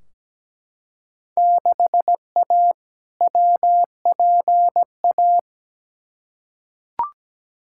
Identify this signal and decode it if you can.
Morse code, heard as a keyed tone, about 17 wpm, 700 Hz. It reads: 6A WPA
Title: Field Day Exchanges Encore - 17wpm